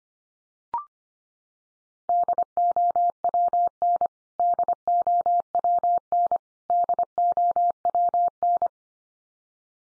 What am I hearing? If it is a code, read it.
Morse code, heard as a keyed tone, about 25 wpm, 700 Hz. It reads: DOWN DOWN DOWN
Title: Top 100 Words 25wpm x3